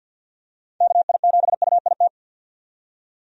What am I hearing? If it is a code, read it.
Morse code, heard as a keyed tone, about 45 wpm, 700 Hz. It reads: KI7UIT
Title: USA Call Signs - 6 Characters - 50wpm